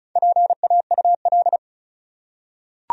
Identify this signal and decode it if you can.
Morse code, heard as a keyed tone, about 34 wpm, 700 Hz. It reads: PAUL
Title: USA Common Names - Max 4 Letters - 35wpm